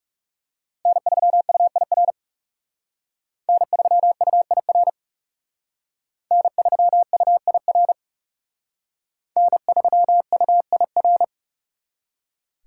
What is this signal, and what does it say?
Morse code, heard as a keyed tone, about 37 wpm, 700 Hz. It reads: N3UIR N3UIR N3UIR N3UIR
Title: Speed Racing - USA Call Signs 30wpm